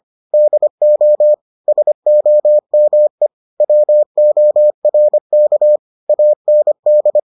Morse code, heard as a keyed tone, about 25 wpm, 600 Hz. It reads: DO SOME WORK AND